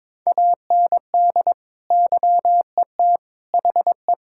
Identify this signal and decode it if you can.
Morse code, heard as a keyed tone, about 22 wpm, 700 Hz. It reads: AND YET HE
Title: Sentences from Top 1600 Words 22wpm